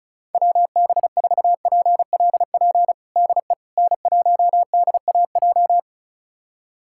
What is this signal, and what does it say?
Morse code, heard as a keyed tone, about 36 wpm, 700 Hz. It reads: WB4PLP DE N1DAJ
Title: USA Callsign DE Callsign 35wpm